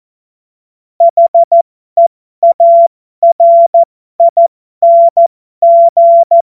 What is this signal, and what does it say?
Morse code, heard as a keyed tone, about 14 wpm, spaced as for 12 wpm, 680 Hz. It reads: HEARING
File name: Morsekurs / Witze